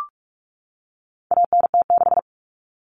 Morse code, heard as a keyed tone, about 44 wpm, 700 Hz. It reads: ANT6
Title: Common Cut Numbers - 4-Digit 45wpm